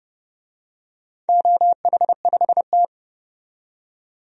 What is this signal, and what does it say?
Morse code, heard as a keyed tone, about 32 wpm, 700 Hz. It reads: OH5T
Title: DXCC-100 Call Signs - 4 Characters - 30wpm